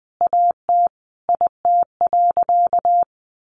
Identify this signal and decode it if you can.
Morse code, heard as a keyed tone, about 20 wpm, 700 Hz. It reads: AT IT.
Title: Sentences from Top 100 Words 20wpm